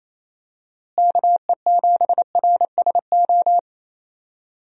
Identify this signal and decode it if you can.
Morse code, heard as a keyed tone, about 28 wpm, 700 Hz. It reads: KE7RSO
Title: USA Call Signs - 6 Characters - Repeated 3x - 28wpm